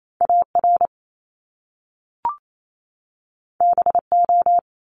Morse code, heard as a keyed tone, about 28 wpm, 700 Hz. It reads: AR BO
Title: Top Two Letter Combinations in Words 28wpm